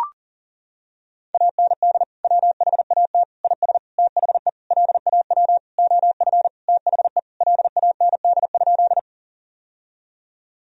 Morse code, heard as a keyed tone, about 40 wpm, 700 Hz. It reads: AND WHAT IS THE LAW OF THE LAND?
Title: Sentences from Top 1000 Words - Review Part 3 - 40wpm